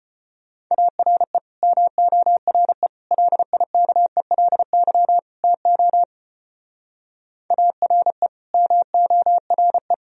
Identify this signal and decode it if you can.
Morse code, heard as a keyed tone, about 31 wpm, 700 Hz. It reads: ARE MORE LIKELY TO ARE MORE